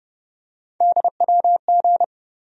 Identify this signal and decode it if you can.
Morse code, heard as a keyed tone, about 31 wpm, 700 Hz. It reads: DWG
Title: USA Call Sign Postfixes - 3 Letters 30wpm